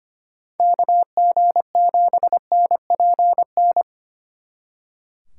Morse code, heard as a keyed tone, about 29 wpm, 700 Hz. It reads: KG7NPN
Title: Speed Racing - USA 2x3 Call Signs 25wpm